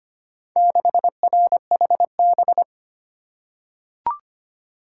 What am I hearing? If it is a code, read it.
Morse code, heard as a keyed tone, about 25 wpm, 700 Hz. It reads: 6RHB
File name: practice / 25wpm / Four Letter-Number Combinations - 25wpm